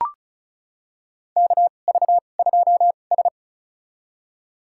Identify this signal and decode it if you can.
Morse code, heard as a keyed tone, about 36 wpm, spaced as for 25 wpm, 700 Hz. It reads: KV2S
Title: USA Call Signs - 2x Character Spacing - 35wpm